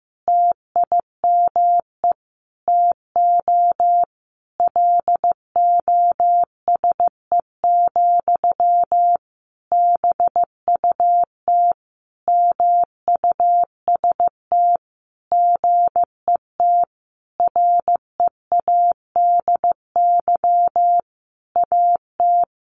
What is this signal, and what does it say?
Morse code, heard as a keyed tone, about 15 wpm, 700 Hz. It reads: TIME TO LOSE, BUT MUST GET READY AT
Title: Sentences from Top 1100 Words 15wpm